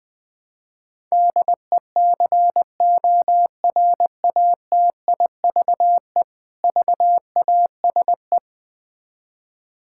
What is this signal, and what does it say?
Morse code, heard as a keyed tone, about 21 wpm, 700 Hz. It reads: DECORATIVE VASE
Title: Difficult Words with Hints - 20wpm